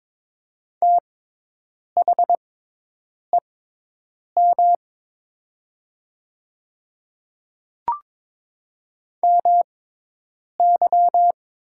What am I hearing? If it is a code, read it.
Morse code, heard as a keyed tone, about 22 wpm, spaced as for 8 wpm, 700 Hz. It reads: THEM MY
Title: 6x Character Spacing - Top 100 Words - 22wpm